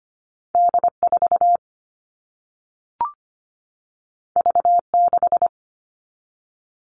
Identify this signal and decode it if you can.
Morse code, heard as a keyed tone, about 25 wpm, 700 Hz. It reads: D4 V6
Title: All Cut Numbers - 2-Digit 25wpm